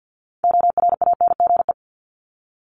Morse code, heard as a keyed tone, about 50 wpm, 700 Hz. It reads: ORANGE
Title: Top 1751-2000 Words 50wpm